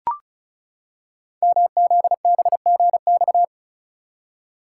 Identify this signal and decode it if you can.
Morse code, heard as a keyed tone, about 36 wpm, 690 Hz. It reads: MZBGX
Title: Code Groups - 5 Letters - 35wpm